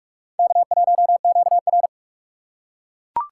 Morse code, heard as a keyed tone, about 44 wpm, 700 Hz. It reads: K1QR